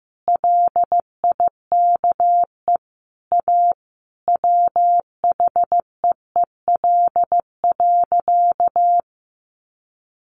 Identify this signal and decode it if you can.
Morse code, heard as a keyed tone, about 15 wpm, 700 Hz. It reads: LIKE A WHEEL.
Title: Sentences from Top 500 Words 15wpm